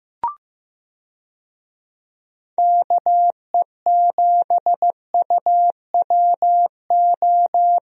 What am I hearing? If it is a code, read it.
Morse code, heard as a keyed tone, about 16 wpm, 700 Hz. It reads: KE7UWO
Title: USA Call Signs - 6 Characters - Repeated 3x - 15wpm